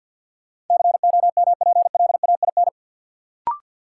Morse code, heard as a keyed tone, about 47 wpm, 700 Hz. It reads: COMPLAIN
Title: Sentences from Top 1300 Words - New Word Drill - 50wpm